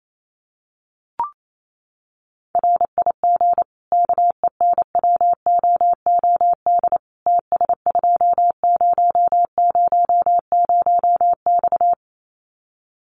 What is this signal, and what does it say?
Morse code, heard as a keyed tone, about 28 wpm, 700 Hz. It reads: RIG KENWOOD TS2000X
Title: Common QSO Phrases 28wpm